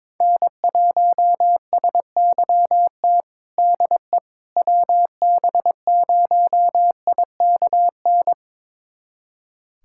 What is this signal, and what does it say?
Morse code, heard as a keyed tone, about 22 wpm, 700 Hz. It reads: N1SYT DE WB0IKN
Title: USA Callsign DE Callsign 22wpm